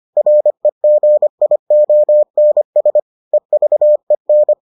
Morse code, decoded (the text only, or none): REGIONS EVEN